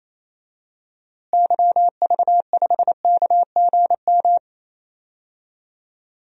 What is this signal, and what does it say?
Morse code, heard as a keyed tone, about 27 wpm, 700 Hz. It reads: YV5KGM
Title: DXCC-100 Call Signs - 6 Characters - Repeated 3x - 28wpm